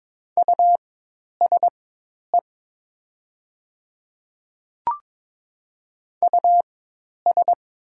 Morse code, heard as a keyed tone, about 22 wpm, spaced as for 10 wpm, 700 Hz. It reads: USE US
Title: 4x Character Spacing - Top 300 Words - 22wpm